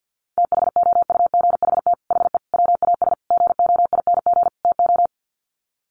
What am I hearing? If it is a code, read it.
Morse code, heard as a keyed tone, about 50 wpm, 700 Hz. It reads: THOUGHT HE WAS GOING TO
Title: Sets of 5 Words Encore2 50wpm